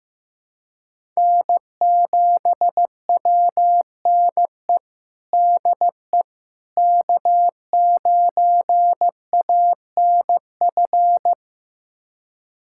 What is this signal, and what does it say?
Morse code, heard as a keyed tone, about 16 wpm, 700 Hz. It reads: N7WNE DE K9ANF
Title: USA Callsign DE Callsign - 5 Characters - 15wpm